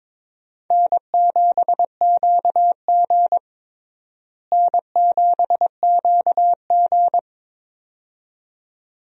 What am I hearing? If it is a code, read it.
Morse code, heard as a keyed tone, about 22 wpm, 700 Hz. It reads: N7QG N7QG
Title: USA Call Signs - 4 Characters - Repeated 2x - 22wpm